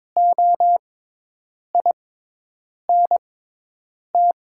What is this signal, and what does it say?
Morse code, heard as a keyed tone, about 22 wpm, spaced as for 8 wpm, 700 Hz. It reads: OINT
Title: 6x Character Spacing - Top 300 Words - 22wpm